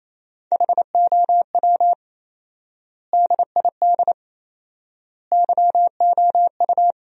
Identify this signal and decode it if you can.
Morse code, heard as a keyed tone, about 28 wpm, 700 Hz. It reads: HOW DID YOU
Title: 4x Word-Spacing - Sentences from Top 200 Words - 28wpm